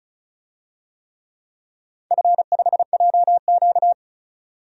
Morse code, heard as a keyed tone, about 36 wpm, 700 Hz. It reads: F5JQ
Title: DXCC-100 Call Signs - 4 Characters - 35wpm